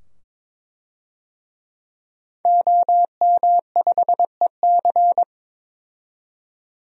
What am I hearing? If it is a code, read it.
Morse code, heard as a keyed tone, about 22 wpm, 700 Hz. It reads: OM5EC
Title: DXCC-100 Call Signs - 5 Characters - 22wpm